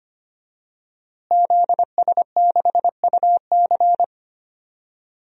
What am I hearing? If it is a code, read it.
Morse code, heard as a keyed tone, about 25 wpm, 700 Hz. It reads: ZS6UC
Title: DXCC-100 Call Signs - 5 Characters - Repeated 2x - 25wpm